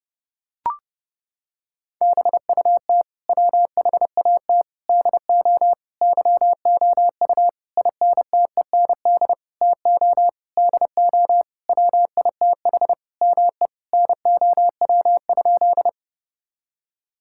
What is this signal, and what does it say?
Morse code, heard as a keyed tone, about 30 wpm, 700 Hz. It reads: BUT WHAT DO YOU INTEND TO DO WITH ME NOW?
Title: Sentences from Top 1600 Words 30wpm